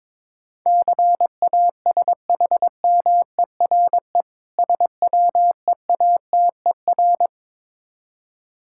Morse code, heard as a keyed tone, about 22 wpm, 690 Hz. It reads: CASHMERE SWEATER
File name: Difficult Words with Hints - 22wpm